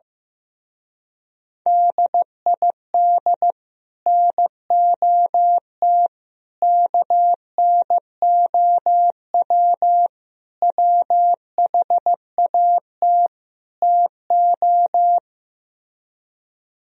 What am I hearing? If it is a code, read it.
Morse code, heard as a keyed tone, about 15 wpm, 700 Hz. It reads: DID NOT KNOW WHAT TO
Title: Speed-Racing - Sets of 5 Words 15wpm